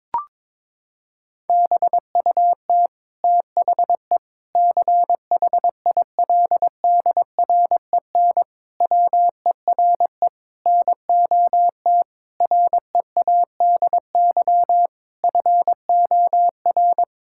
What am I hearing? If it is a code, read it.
Morse code, heard as a keyed tone, about 22 wpm, 700 Hz. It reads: BUT THE CHILDREN WERE NOT READY FOR
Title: Sentences from Top 1000 Words - Review Part 6 - 22wpm